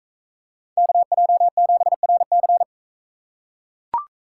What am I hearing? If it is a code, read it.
Morse code, heard as a keyed tone, about 41 wpm, 700 Hz. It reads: KJ7RC